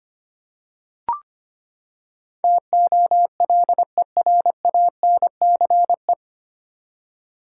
Morse code, heard as a keyed tone, about 25 wpm, 700 Hz. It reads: TOLERANCE
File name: Nine Letter Words - 25wpm